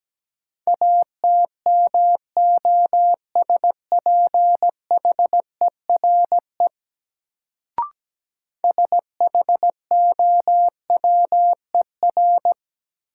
Morse code, heard as a keyed tone, about 17 wpm, 700 Hz. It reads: ATMOSPHERE SHOWER
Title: Sentences from Top 1600 Words - New Word Drill Alt - 17wpm